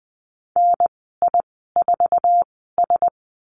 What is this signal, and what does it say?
Morse code, heard as a keyed tone, about 20 wpm, spaced as for 15 wpm, 700 Hz. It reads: NI4S
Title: 048 Lesson 9g - Number 4 - Call Signs 20wpm